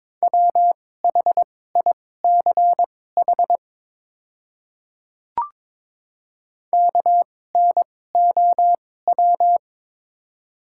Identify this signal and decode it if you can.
Morse code, heard as a keyed tone, about 22 wpm, spaced as for 16 wpm, 700 Hz. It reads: WHICH KNOW
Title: 2x Character Spacing - Top 200 Words - 22wpm